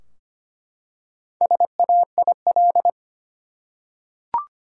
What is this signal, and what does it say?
Morse code, heard as a keyed tone, about 25 wpm, 700 Hz. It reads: SAIL
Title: Four Letter Words - 25wpm